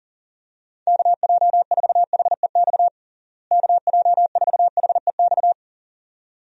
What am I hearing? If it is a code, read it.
Morse code, heard as a keyed tone, about 40 wpm, 700 Hz. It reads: KJ4HEX KJ4HEX
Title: USA Call Signs - 6 Characters - Repeated 2x - 40wpm